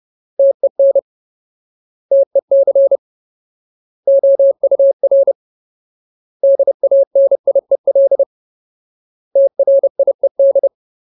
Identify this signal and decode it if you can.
Morse code, heard as a keyed tone, about 30 wpm, 550 Hz. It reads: TEN TEC OUR DANIEL TRIED
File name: Ultimate Word List / 30WPM Ultimate